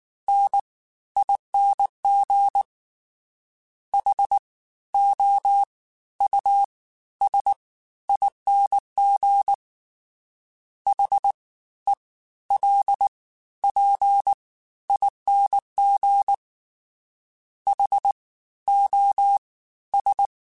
Morse code, heard as a keyed tone, about 19 wpm, spaced as for 13 wpm, 800 Hz. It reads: NING HOUSING HELPING HOS